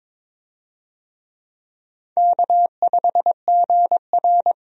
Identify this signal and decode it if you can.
Morse code, heard as a keyed tone, about 22 wpm, 700 Hz. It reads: K5GR
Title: USA Call Signs - 4 Characters - 22wpm